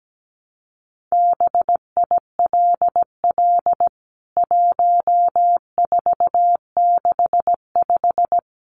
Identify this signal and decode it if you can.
Morse code, heard as a keyed tone, about 17 wpm, 700 Hz. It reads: BILL 1465
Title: CWT Exchange 17wpm